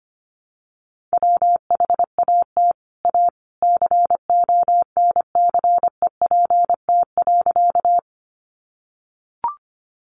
Morse code, heard as a keyed tone, about 25 wpm, 700 Hz. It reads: WHAT A CONCEPT.
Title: Sentences from Top 1100 Words 25wpm